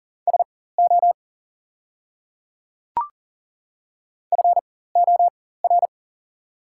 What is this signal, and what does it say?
Morse code, heard as a keyed tone, about 40 wpm, spaced as for 19 wpm, 690 Hz. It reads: SO FOR
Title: 4x Character Spacing - Top 100 Words - 40wpm